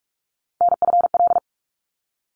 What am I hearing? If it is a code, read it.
Morse code, heard as a keyed tone, about 44 wpm, 700 Hz. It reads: NFL
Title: ARRL Sections 45wpm